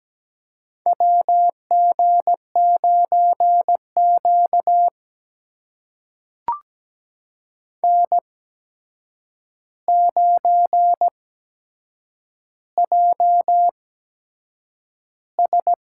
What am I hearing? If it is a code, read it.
Morse code, heard as a keyed tone, about 17 wpm, spaced as for 7 wpm, 700 Hz. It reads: WG9Q N9JS
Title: USA Call Signs - 4 Characters - 8x Character Spacing - 17wpm